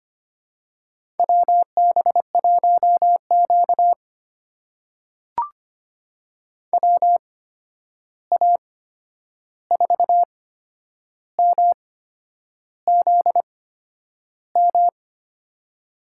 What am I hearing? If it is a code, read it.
Morse code, heard as a keyed tone, about 25 wpm, spaced as for 9 wpm, 700 Hz. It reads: WB1Q WA4MZM
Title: USA Call Signs - 8x Character Spacing - 25wpm